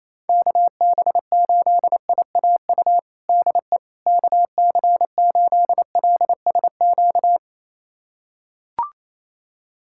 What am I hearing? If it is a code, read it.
Morse code, heard as a keyed tone, about 28 wpm, 700 Hz. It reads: KB8IAU DE KC8LSQ
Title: USA Callsign DE Callsign - 6 Characters - 28wpm